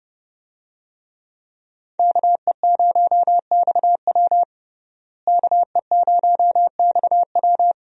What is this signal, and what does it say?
Morse code, heard as a keyed tone, about 31 wpm, 700 Hz. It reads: KE0XW KE0XW
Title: USA Call Signs - 5 Characters - Repeated 2x - 30wpm